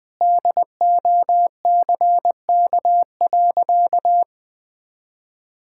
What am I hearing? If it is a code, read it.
Morse code, heard as a keyed tone, about 21 wpm, 700 Hz. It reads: DOCK.
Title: Sentences with 2-Word Phrases - 20wpm